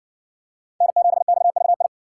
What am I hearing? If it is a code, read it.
Morse code, heard as a keyed tone, about 53 wpm, 700 Hz. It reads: N7XVI